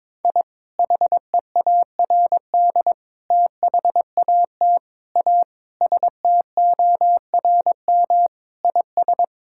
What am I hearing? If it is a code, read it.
Morse code, heard as a keyed tone, about 21 wpm, 700 Hz. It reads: I HEARD THAT A STORM IS
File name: practice / 22wpm / Sentences with 2-Word Phrases - 22wpm